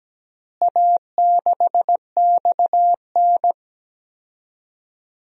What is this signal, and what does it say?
Morse code, heard as a keyed tone, about 17 wpm, 700 Hz. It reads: A6XN
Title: DXCC-100 Call Signs - 4 Characters - Repeated 3x - 17wpm